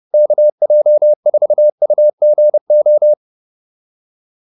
Morse code, heard as a keyed tone, about 30 wpm, 600 Hz. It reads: KJ4UGO